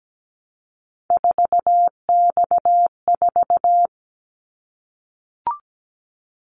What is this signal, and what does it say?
Morse code, heard as a keyed tone, about 17 wpm, 700 Hz. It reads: 4X4
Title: Callsign Prefixes for Top 50 DXCC - 17wpm